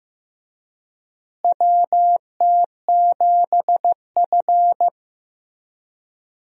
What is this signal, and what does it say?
Morse code, heard as a keyed tone, about 15 wpm, 700 Hz. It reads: WT7F
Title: USA Call Signs - 4 Characters - Repeated 3x - 15wpm